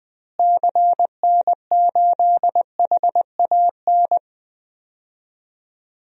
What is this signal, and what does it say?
Morse code, heard as a keyed tone, about 21 wpm, 700 Hz. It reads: CN8HAN
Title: DXCC-100 Call Signs - 6 Characters - Repeated 2x - 20wpm